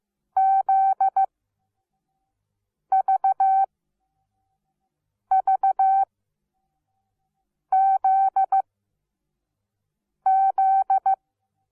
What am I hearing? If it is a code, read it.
Morse code, heard as a keyed tone, about 15 wpm, spaced as for 5 wpm, 770 Hz. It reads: ZVVZZ
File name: Onlinelehrgang / morsen / DJ4UF Morsen17